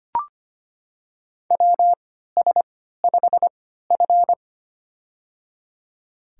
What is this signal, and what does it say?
Morse code, heard as a keyed tone, about 25 wpm, spaced as for 14 wpm, 700 Hz. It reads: WS5F
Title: USA Call Signs - 3x Character Spacing - 25wpm